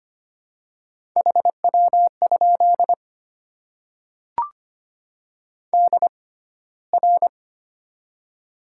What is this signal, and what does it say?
Morse code, heard as a keyed tone, about 25 wpm, spaced as for 15 wpm, 690 Hz. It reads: HW? DR
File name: Top 90 QSO Elements - 6x Character Spacing - 25wpm